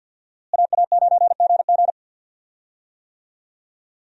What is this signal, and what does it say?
Morse code, heard as a keyed tone, about 47 wpm, 700 Hz. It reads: AA9GG
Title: USA Call Signs - 5 Characters - Repeated 2x - 50wpm